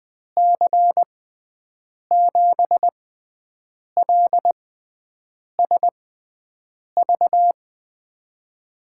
Morse code, heard as a keyed tone, about 20 wpm, spaced as for 7 wpm, 700 Hz. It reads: C7LSV
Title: USA Call Signs - 6 Characters - 6x Character Spacing - 20wpm